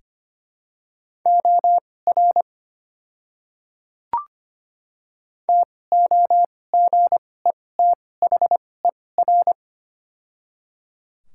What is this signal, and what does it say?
Morse code, heard as a keyed tone, about 25 wpm, spaced as for 18 wpm, 700 Hz. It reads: OR TOGETHER
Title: 2x Character Spacing - Top 300 Words - 25wpm